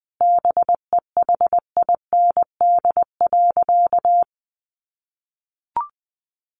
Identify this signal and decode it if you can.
Morse code, heard as a keyed tone, about 20 wpm, 700 Hz. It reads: BEHIND.
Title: Sentences from Top 700 Words 20wpm